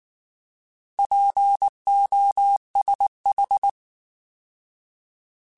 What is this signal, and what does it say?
Morse code, heard as a keyed tone, about 19 wpm, 800 Hz. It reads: POSH